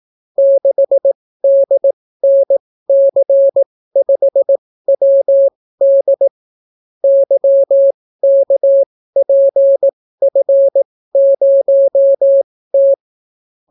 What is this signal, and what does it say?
Morse code, heard as a keyed tone, about 18 wpm, spaced as for 14 wpm, 550 Hz. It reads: 6DNC5WD YKPF0T